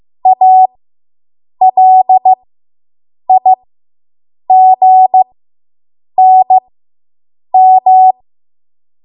Morse code, heard as a keyed tone, about 15 wpm, spaced as for 7 wpm, 750 Hz. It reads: ALIGNM